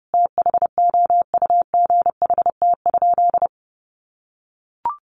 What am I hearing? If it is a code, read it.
Morse code, heard as a keyed tone, about 30 wpm, 700 Hz. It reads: THOUGHT?
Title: Sentences from Top 300 Words 30wpm